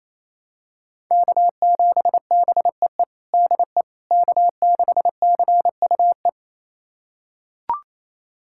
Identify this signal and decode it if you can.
Morse code, heard as a keyed tone, about 27 wpm, 700 Hz. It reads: K7BEE DE K6CUE
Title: USA Callsign DE Callsign - 5 Characters - 28wpm